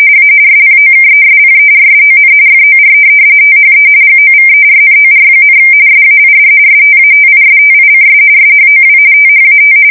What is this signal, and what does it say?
RTTY, about 45.45 baud. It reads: A SPOKESPERSON FOR THE GOVERNOR, SAID STATISTICS SHOW THAT
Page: 170 Shift 2125/2295